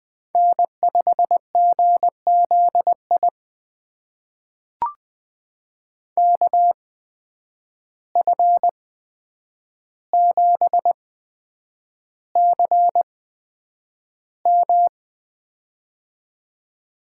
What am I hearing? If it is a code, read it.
Morse code, heard as a keyed tone, about 21 wpm, spaced as for 9 wpm, 700 Hz. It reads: N5GZI KF7CM
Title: USA Call Signs - 5 Characters - 8x Character Spacing - 20wpm